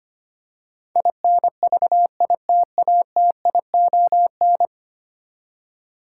Morse code, heard as a keyed tone, about 25 wpm, 700 Hz. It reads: INVITATION
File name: Ten Letter Words - 25wpm